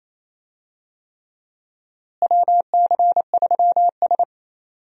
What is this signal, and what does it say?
Morse code, heard as a keyed tone, about 28 wpm, 700 Hz. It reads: WC3S
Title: USA Call Signs - 4 Characters - Repeated 3x - 28wpm